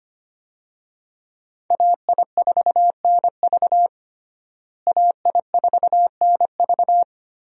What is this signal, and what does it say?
Morse code, heard as a keyed tone, about 25 wpm, 700 Hz. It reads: AI4NV AI4NV
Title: USA Call Signs - 5 Characters - Repeated 2x - 25wpm